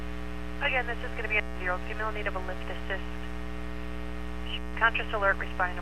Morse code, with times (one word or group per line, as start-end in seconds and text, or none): none